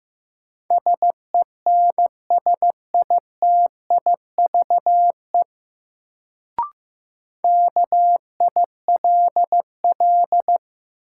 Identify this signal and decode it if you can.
Morse code, heard as a keyed tone, about 15 wpm, 700 Hz. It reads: SENSITIVE KILL
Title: Sentences from Top 1400 Words - New Word Drill - 15wpm